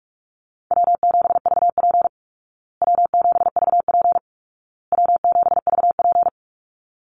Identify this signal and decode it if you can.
Morse code, heard as a keyed tone, about 44 wpm, 700 Hz. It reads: W7VP W7VP W7VP
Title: Easy Call Signs Repeated 3x - 45wpm